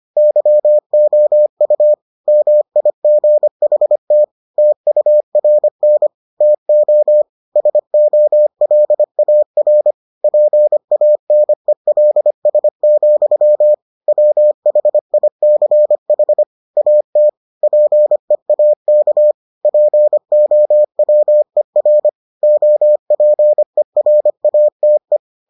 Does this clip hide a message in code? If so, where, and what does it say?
Yes: Morse code from 0.2 to 25.2 s, heard as a keyed tone, about 25 wpm, 600 Hz. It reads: YOU MIGHT TURN TO SOLAR PANELS, WHICH AT PEAK POWER OPERATE